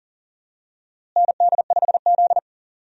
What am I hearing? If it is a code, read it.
Morse code, heard as a keyed tone, about 40 wpm, 700 Hz. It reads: ND5Z
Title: USA Call Signs - 4 Characters - Repeated 2x - 40wpm